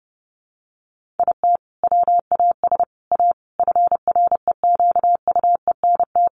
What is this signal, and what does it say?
Morse code, heard as a keyed tone, about 30 wpm, 700 Hz. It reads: IT WAS A FREQUENT
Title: Sentences from Top 1400 Words 30wpm